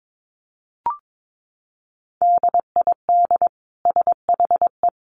Morse code, heard as a keyed tone, about 22 wpm, 700 Hz. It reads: DID SHE
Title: Sentences from Top 1100 Words 22wpm